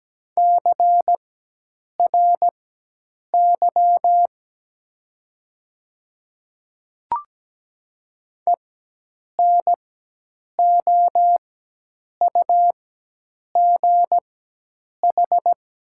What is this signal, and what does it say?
Morse code, heard as a keyed tone, about 17 wpm, spaced as for 8 wpm, 700 Hz. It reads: CRY ENOUGH
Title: 4x Character Spacing - Top 500 Words - 17wpm